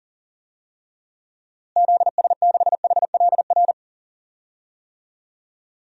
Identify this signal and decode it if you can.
Morse code, heard as a keyed tone, about 40 wpm, 700 Hz. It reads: ZS6HLR
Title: DXCC-100 Call Signs - 6 Characters - Repeated 3x - 40wpm